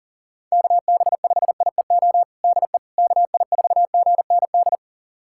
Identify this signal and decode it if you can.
Morse code, heard as a keyed tone, about 40 wpm, 700 Hz. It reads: KB5IEO DE KI4GND
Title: USA Callsign DE Callsign 40wpm